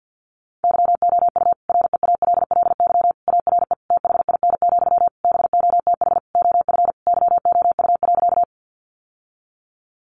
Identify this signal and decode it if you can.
Morse code, heard as a keyed tone, about 50 wpm, 700 Hz. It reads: YOU REALLY ARE THIN, BOTH OF YOU.
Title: Sentences from Top 1200 Words 50wpm